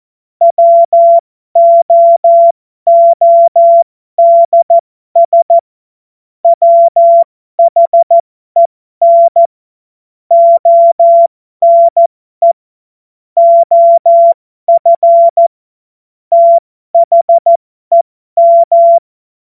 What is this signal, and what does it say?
Morse code, heard as a keyed tone, about 14 wpm, spaced as for 12 wpm, 680 Hz. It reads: WOODS WHEN ONE OF THEM
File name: Morsekurs / Witze